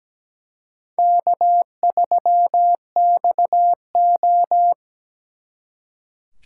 Morse code, heard as a keyed tone, about 20 wpm, 700 Hz. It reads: K3XO
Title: Speed Racing - USA 1x2 Call Signs 17wpm